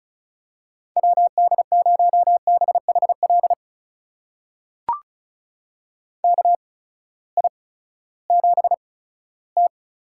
Morse code, heard as a keyed tone, about 36 wpm, spaced as for 18 wpm, 700 Hz. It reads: WD0BHL KI7T
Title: USA Call Signs - 6 Characters - 8x Character Spacing - 35wpm